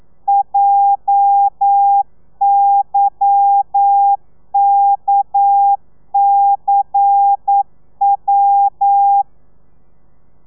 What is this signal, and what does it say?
Morse code, heard as a keyed tone, about 9 wpm, 800 Hz. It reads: JYKCW